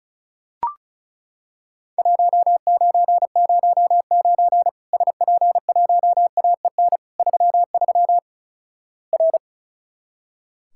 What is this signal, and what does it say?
Morse code, heard as a keyed tone, about 36 wpm, 700 Hz. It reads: 1909 SP1AEN 33 R
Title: Worked All Europe DX Contest - QTC Exchange - 35wpm